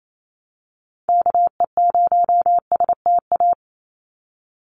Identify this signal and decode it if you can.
Morse code, heard as a keyed tone, about 28 wpm, 700 Hz. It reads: KE0STA
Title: USA Call Signs 28wpm